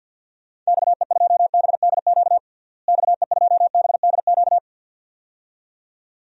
Morse code, heard as a keyed tone, about 52 wpm, 700 Hz. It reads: XE2BDQ XE2BDQ
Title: DXCC-100 Call Signs - 6 Characters - Repeated 2x - 50wpm